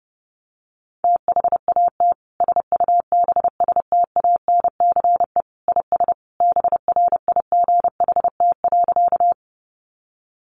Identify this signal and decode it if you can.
Morse code, heard as a keyed tone, about 30 wpm, 700 Hz. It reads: THAT SUBSTANCE IS BRIGHT.
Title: Sentences from Top 1500 Words 30wpm